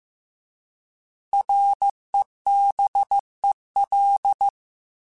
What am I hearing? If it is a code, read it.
Morse code, heard as a keyed tone, about 15 wpm, 800 Hz. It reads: REBEL